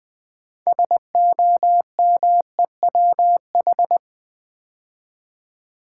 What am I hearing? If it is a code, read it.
Morse code, heard as a keyed tone, about 20 wpm, 700 Hz. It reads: SOMEWH
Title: Build A Word - 3 Letters At A Time - 8-Letter Words - 20wpm